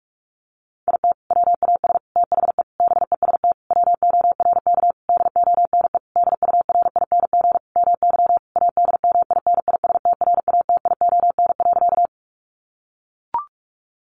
Morse code, heard as a keyed tone, about 44 wpm, 700 Hz. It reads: IT WAS THE BEST WORK DONE DURING MY ADMINISTRATION.
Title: Sentences from Top 1300 Words 45wpm